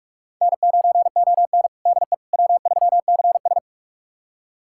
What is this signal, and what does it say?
Morse code, heard as a keyed tone, about 47 wpm, 700 Hz. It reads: N9ON DE W3CS
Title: USA Callsign DE Callsign - 4 Characters - 45wpm